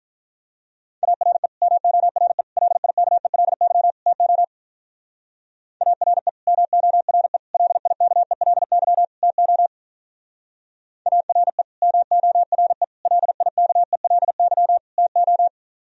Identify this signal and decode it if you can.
Morse code, heard as a keyed tone, about 43 wpm, 700 Hz. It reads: ARE MORE LIKELY TO ARE MORE LIKELY TO ARE MORE LIKELY TO
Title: Speed-Racing - Sets of 4 Words 35wpm